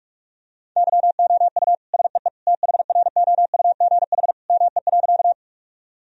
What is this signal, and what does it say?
Morse code, heard as a keyed tone, about 46 wpm, 700 Hz. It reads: YOU SEE THROUGH ME.
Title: Sentences from Top 1000 Words - Review Part 1 - 45wpm